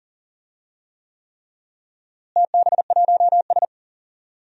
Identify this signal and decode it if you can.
Morse code, heard as a keyed tone, about 37 wpm, 700 Hz. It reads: TB1S
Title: DXCC-100 Call Signs - 4 Characters - Repeated 3x - 40wpm